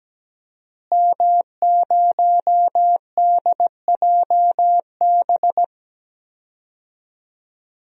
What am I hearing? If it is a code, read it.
Morse code, heard as a keyed tone, about 17 wpm, 700 Hz. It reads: M0DJB
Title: DXCC-100 Call Signs - 5 Characters - Repeated 3x - 17wpm